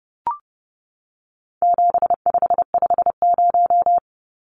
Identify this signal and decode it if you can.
Morse code, heard as a keyed tone, about 30 wpm, 700 Hz. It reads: 7550
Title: Numbers - 4-Digit 30wpm